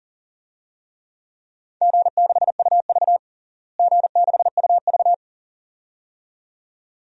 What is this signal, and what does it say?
Morse code, heard as a keyed tone, about 40 wpm, 700 Hz. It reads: G6UV G6UV
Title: DXCC-100 Call Signs - 4 Characters - Repeated 2x - 40wpm